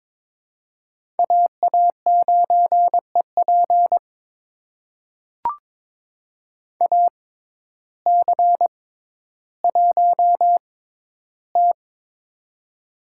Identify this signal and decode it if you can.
Morse code, heard as a keyed tone, about 22 wpm, spaced as for 12 wpm, 700 Hz. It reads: AA9EP AC1T
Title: USA Call Signs - 6x Character Spacing - 22wpm